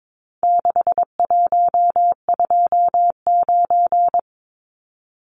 Morse code, heard as a keyed tone, about 22 wpm, 700 Hz. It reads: 6129
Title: Numbers - 4-Digit 22wpm